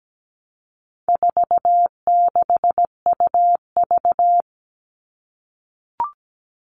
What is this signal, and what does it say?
Morse code, heard as a keyed tone, about 17 wpm, 700 Hz. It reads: 46UV
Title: All Cut Numbers - 4-Digit 17wpm